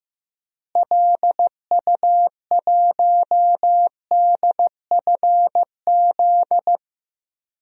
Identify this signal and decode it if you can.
Morse code, heard as a keyed tone, about 16 wpm, 700 Hz. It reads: LU1DFZ
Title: DXCC-100 Call Signs - 6 Characters - Repeated 2x - 15wpm